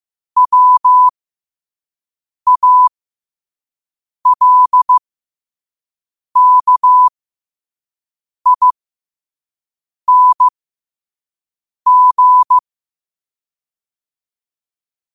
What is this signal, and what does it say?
Morse code, heard as a keyed tone, about 15 wpm, spaced as for 5 wpm, 1000 Hz. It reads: WALKING
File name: 5wpm